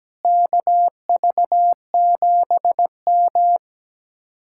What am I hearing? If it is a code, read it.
Morse code, heard as a keyed tone, about 17 wpm, 700 Hz. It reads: KV7M
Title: Speed-Racing Common QSO Phrases Encore - 17wpm